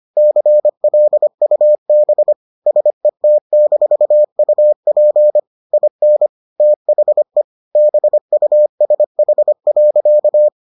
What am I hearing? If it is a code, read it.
Morse code, heard as a keyed tone, about 25 wpm, 600 Hz. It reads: CLUB SET-UP IN THE BUSH.